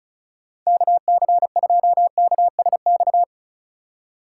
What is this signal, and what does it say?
Morse code, heard as a keyed tone, about 36 wpm, 700 Hz. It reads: KC2KSX
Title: USA Call Signs - 6 Characters - 4x Character Spacing - 35wpm